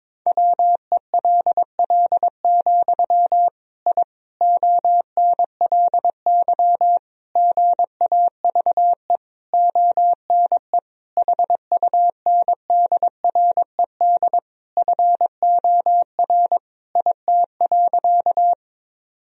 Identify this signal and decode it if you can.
Morse code, heard as a keyed tone, about 22 wpm, 700 Hz. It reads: WELL, I ONLY GAVE ONE HUNDRED FOR IT.
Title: Sentences from Top 1000 Words - Review Part 2 - 22wpm